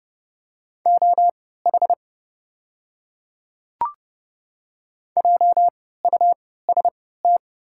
Morse code, heard as a keyed tone, about 29 wpm, spaced as for 17 wpm, 700 Hz. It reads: OH JUST
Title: 3x Character Spacing - Top 500 Words - 30wpm